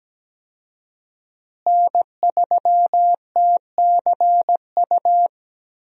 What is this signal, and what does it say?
Morse code, heard as a keyed tone, about 17 wpm, 700 Hz. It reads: N3TCU
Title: USA Call Signs - 5 Characters - Repeated 2x - 17wpm